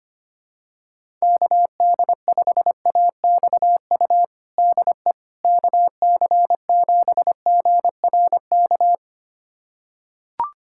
Morse code, heard as a keyed tone, about 25 wpm, 700 Hz. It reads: KD5AXU DE KC7GRK
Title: USA Callsign DE Callsign - 6 Characters - 25wpm